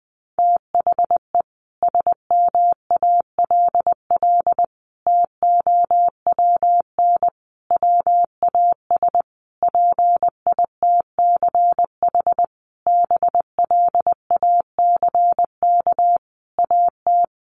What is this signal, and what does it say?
Morse code, heard as a keyed tone, about 20 wpm, 700 Hz. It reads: THE SMALL TOWN WAS PITCH BLACK AT
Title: Sentences from Top 1300 Words 20wpm